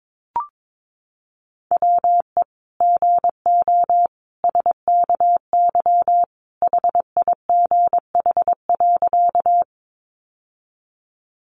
Morse code, heard as a keyed tone, about 22 wpm, 700 Hz. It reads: WE GO SKY HIGH.
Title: Sentences from Top 1200 Words 22wpm